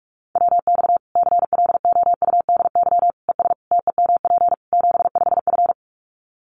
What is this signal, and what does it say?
Morse code, heard as a keyed tone, about 44 wpm, 700 Hz. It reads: WX CLOUDY ES TEMP 75F
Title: Common QSO Phrases 45wpm